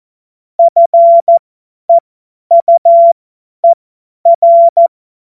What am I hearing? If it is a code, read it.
Morse code, heard as a keyed tone, about 14 wpm, spaced as for 10 wpm, 680 Hz. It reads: FEUER
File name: Morsekurs / Witze / Witz 09. November 2021, 50 F70